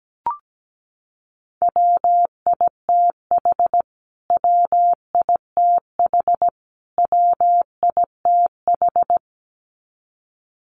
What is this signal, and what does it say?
Morse code, heard as a keyed tone, about 17 wpm, 700 Hz. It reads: WITH WITH WITH
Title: Top 200 Words Repeated 17wpm